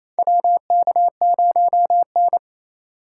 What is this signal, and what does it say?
Morse code, heard as a keyed tone, about 28 wpm, 700 Hz. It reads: WK0N
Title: USA Call Signs - 4 Characters - 2x Character Spacing - 28wpm